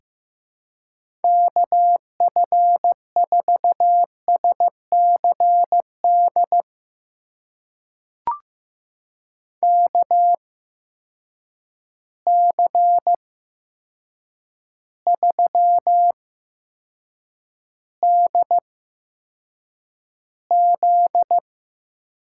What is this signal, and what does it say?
Morse code, heard as a keyed tone, about 15 wpm, spaced as for 7 wpm, 700 Hz. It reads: KF4SCD KC3DZ
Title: USA Call Signs - 6 Characters - 8x Character Spacing - 15wpm